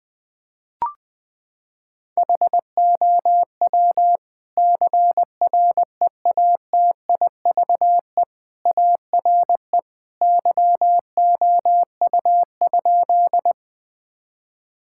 Morse code, heard as a keyed tone, about 20 wpm, 700 Hz. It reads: HOW CREATIVE ARE YOU?
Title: Sentences from Top 1000 Words - Review Part 5 - 20wpm